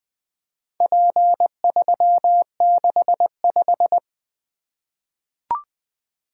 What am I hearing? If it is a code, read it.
Morse code, heard as a keyed tone, about 20 wpm, 700 Hz. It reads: P365
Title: Four Letter-Number Combinations - 20wpm